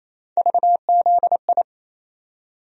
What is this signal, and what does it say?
Morse code, heard as a keyed tone, about 28 wpm, 700 Hz. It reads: VZI
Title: USA Call Sign Postfixes - 3 Letters 28wpm